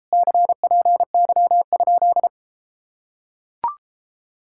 Morse code, heard as a keyed tone, about 33 wpm, 700 Hz. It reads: CPY?